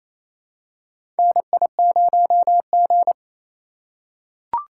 Morse code, heard as a keyed tone, about 28 wpm, 700 Hz. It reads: NI0G
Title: USA Call Signs - 4 Characters - 2x Character Spacing - 28wpm